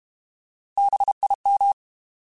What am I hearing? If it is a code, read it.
Morse code, heard as a keyed tone, about 32 wpm, 800 Hz. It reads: BIM
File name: Alphabetical / Alphabetical mp3s / Bs @30